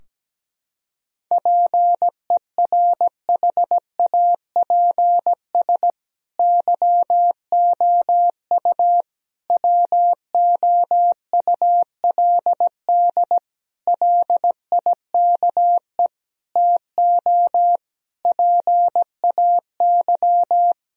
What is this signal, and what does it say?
Morse code, heard as a keyed tone, about 17 wpm, 700 Hz. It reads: PERHAPS YOU WOULD LIKE TO PAY
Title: Sentences from Top 1000 Words - Review Part 2 - 17wpm